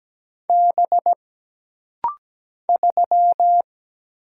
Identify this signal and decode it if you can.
Morse code, heard as a keyed tone, about 17 wpm, 700 Hz. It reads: B 3
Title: Single Letter-Number - Rapid-Fire 17wpm